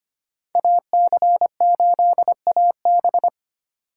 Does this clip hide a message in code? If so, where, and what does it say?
Yes: Morse code from 0.6 to 3.3 s, heard as a keyed tone, about 33 wpm, 700 Hz. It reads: AC8AB